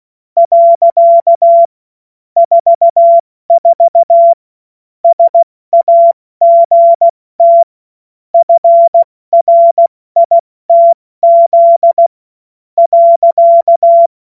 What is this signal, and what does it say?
Morse code, heard as a keyed tone, about 16 wpm, spaced as for 13 wpm, 680 Hz. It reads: . 44 SAGT FRITZ.
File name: Morsekurs / Witze / Witz 13. September 2022 70 F80